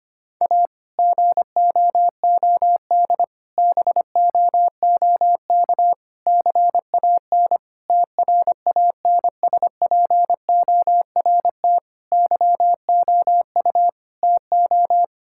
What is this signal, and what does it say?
Morse code, heard as a keyed tone, about 25 wpm, 700 Hz. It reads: A GOOD BOOK CAN TRANSPORT YOU TO
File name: practice / 25wpm / Difficult Sentences with Hints - 25wpm